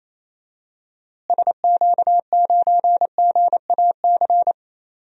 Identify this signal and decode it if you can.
Morse code, heard as a keyed tone, about 27 wpm, 700 Hz. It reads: SQ9GAC
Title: DXCC-100 Call Signs - 6 Characters - Repeated 3x - 28wpm